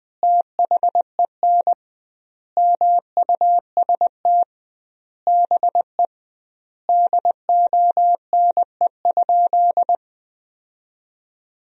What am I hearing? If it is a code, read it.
Morse code, heard as a keyed tone, about 21 wpm, 700 Hz. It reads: THEN MUST BE DONE?
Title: 2x Word-Spacing - Sentences from Top 500 Words - 20wpm